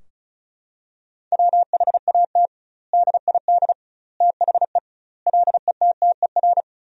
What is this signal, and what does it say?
Morse code, heard as a keyed tone, about 34 wpm, 700 Hz. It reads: WHAT DID THE LETTER
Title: 2x Word-Spacing - Sentences from Top 500 Words - 35wpm